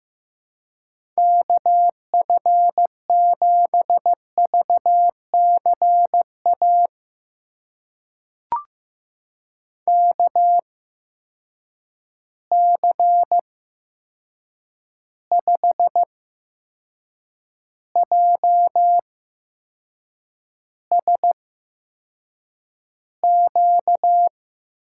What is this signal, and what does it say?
Morse code, heard as a keyed tone, about 15 wpm, spaced as for 6 wpm, 700 Hz. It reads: KF7VCA KC5JSQ
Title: USA Call Signs - 8x Character Spacing - 15wpm